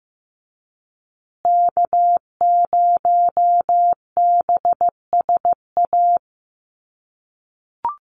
Morse code, heard as a keyed tone, about 15 wpm, 700 Hz. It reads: K0BSA
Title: USA Call Signs Repeated 3x 15wpm